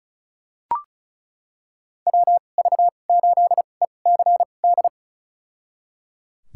Morse code, heard as a keyed tone, about 36 wpm, spaced as for 25 wpm, 700 Hz. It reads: WV8ECD
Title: USA Call Signs - 2x Character Spacing - 35wpm